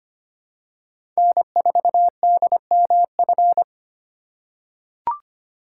Morse code, heard as a keyed tone, about 25 wpm, 700 Hz. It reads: N4DMF
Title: USA Call Signs - 6x Character Spacing - 25wpm